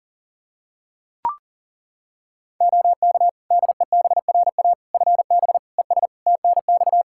Morse code, heard as a keyed tone, about 40 wpm, 700 Hz. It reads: OK DEBRA FB ES TNX
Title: Practice QSO - Exchange 2 - Info - 40wpm